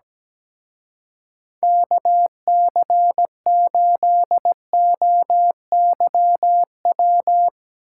Morse code, heard as a keyed tone, about 18 wpm, 700 Hz. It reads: KC8OYW
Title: Speed Racing - USA 2x3 Call Signs 17wpm